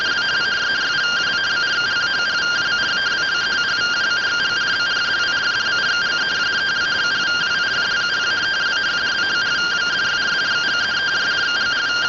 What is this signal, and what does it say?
Two-tone teleprinter signal, about 170 Hz shift.
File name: findings / 81 axial iodine 262.gen-21 rtty rx